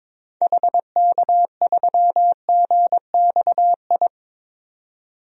Morse code, heard as a keyed tone, about 22 wpm, 700 Hz. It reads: HK3GXI
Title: DXCC-100 Call Signs - 6 Characters - 22wpm